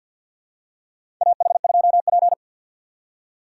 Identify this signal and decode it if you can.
Morse code, heard as a keyed tone, about 46 wpm, 700 Hz. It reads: AH2P
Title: USA Call Signs - 3x Character Spacing - 50wpm